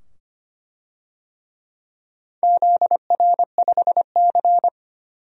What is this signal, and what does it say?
Morse code, heard as a keyed tone, about 25 wpm, 700 Hz. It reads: ZR5C
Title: DXCC-100 Call Signs - 4 Characters - 25wpm